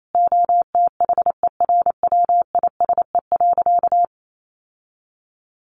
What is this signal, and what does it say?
Morse code, heard as a keyed tone, about 28 wpm, 700 Hz. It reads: OTHERWISE.
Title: Sentences from Top 1500 Words 28wpm